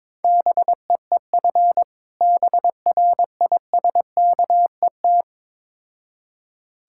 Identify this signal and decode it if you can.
Morse code, heard as a keyed tone, about 22 wpm, 700 Hz. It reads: BEEF BRISKET
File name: Difficult Words with Hints - 22wpm